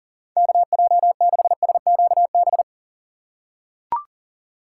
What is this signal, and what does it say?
Morse code, heard as a keyed tone, about 40 wpm, 700 Hz. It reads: KJ6SQB
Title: USA Call Signs - 6 Characters - 2x Character Spacing - 40wpm